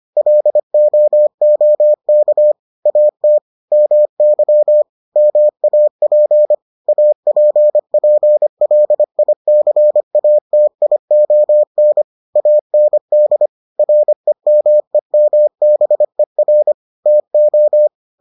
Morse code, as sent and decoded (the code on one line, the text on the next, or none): .-.. --- --- -.- / .- - / -- -.-- / -- .- .--. / .- .--. .--. .-.. .. -.-. .- - .. --- -. / .- -. -.. / .-. -- . -- -... .-. / - ---
LOOK AT MY MAP APPLICATION AND REMEMBER TO